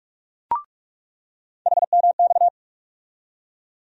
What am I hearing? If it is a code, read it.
Morse code, heard as a keyed tone, about 42 wpm, 700 Hz. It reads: HMX
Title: USA Call Sign Postfixes - 3 Letters 45wpm